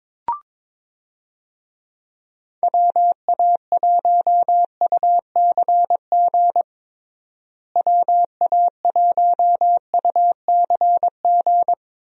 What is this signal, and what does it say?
Morse code, heard as a keyed tone, about 22 wpm, 700 Hz. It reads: WA1UCG WA1UCG
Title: USA Call Signs - 6 Characters - Repeated 2x - 22wpm